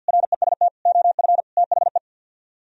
Morse code, heard as a keyed tone, about 60 wpm, 700 Hz. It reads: REST OF THE
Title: Speed-Racing - Sets of 5 Words 50wpm